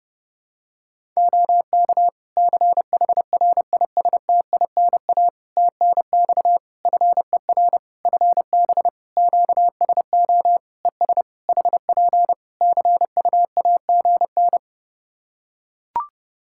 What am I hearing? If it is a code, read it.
Morse code, heard as a keyed tone, about 29 wpm, 700 Hz. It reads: OK CHRISTINA TNX FER FB QSO ES HP CUAGN
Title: Practice QSO - Exchange 4 - Ending - 30wpm